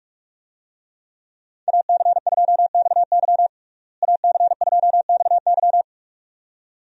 Morse code, heard as a keyed tone, about 41 wpm, 700 Hz. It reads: AC2XY AC2XY
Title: USA Call Signs - 5 Characters - Repeated 2x - 45wpm